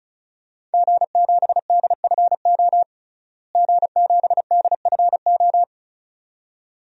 Morse code, heard as a keyed tone, about 36 wpm, 700 Hz. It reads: G7DFO G7DFO
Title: DXCC-100 Call Signs - 5 Characters - Repeated 2x - 35wpm